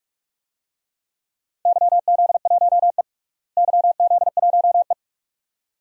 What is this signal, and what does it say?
Morse code, heard as a keyed tone, about 46 wpm, 700 Hz. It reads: YZ1E YZ1E
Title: DXCC-100 Call Signs - 4 Characters - Repeated 2x - 45wpm